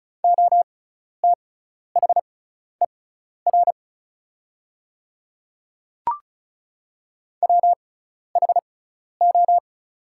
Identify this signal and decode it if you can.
Morse code, heard as a keyed tone, about 34 wpm, spaced as for 12 wpm, 690 Hz. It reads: OTHER WHO
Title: 6x Character Spacing - Top 200 Words - 35wpm